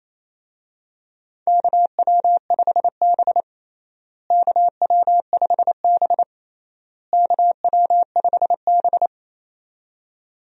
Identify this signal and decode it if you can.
Morse code, heard as a keyed tone, about 28 wpm, 700 Hz. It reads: KW5B KW5B KW5B
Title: USA Call Signs - 4 Characters - Repeated 3x - 28wpm